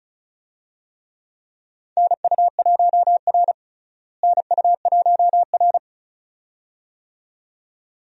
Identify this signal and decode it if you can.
Morse code, heard as a keyed tone, about 36 wpm, 700 Hz. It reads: NU1R NU1R
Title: USA Call Signs - 4 Characters - Repeated 2x - 35wpm